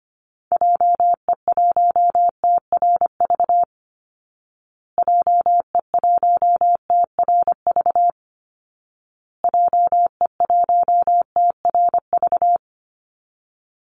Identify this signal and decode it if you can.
Morse code, heard as a keyed tone, about 25 wpm, 700 Hz. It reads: JE1TRV JE1TRV JE1TRV
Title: Call Signs Repeated 2x - 25wpm